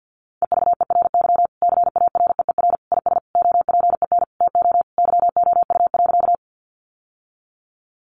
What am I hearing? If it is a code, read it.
Morse code, heard as a keyed tone, about 50 wpm, 700 Hz. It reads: EVERY CAREER IS OPEN TO YOU.
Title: Sentences from Top 1000 Words 50wpm